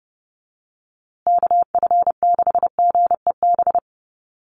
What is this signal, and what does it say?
Morse code, heard as a keyed tone, about 30 wpm, 700 Hz. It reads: KF6GEB
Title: USA Call Signs Repeated 3x 30wpm